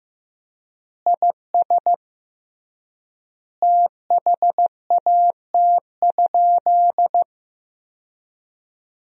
Morse code, heard as a keyed tone, about 15 wpm, 700 Hz. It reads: IS THAT?
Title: 3x Word-Spacing - Sentences from Top 200 Words - 15wpm